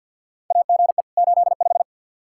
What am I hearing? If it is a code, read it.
Morse code, heard as a keyed tone, about 52 wpm, 700 Hz. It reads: AGE 85
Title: Practice QSO - Exchange 3 - More Info - 50wpm